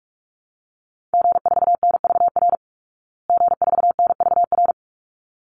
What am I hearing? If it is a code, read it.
Morse code, heard as a keyed tone, about 44 wpm, 700 Hz. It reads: G4NVR G4NVR
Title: Call Signs Repeated 3x - 45wpm